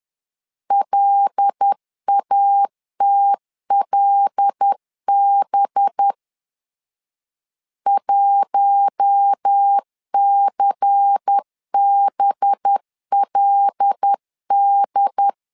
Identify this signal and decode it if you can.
Morse code, heard as a keyed tone, about 10 wpm, 800 Hz. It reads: LATLB 1CBLD
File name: LEZIONI / 5 LBVC / 5 LBVC 10wpm 2349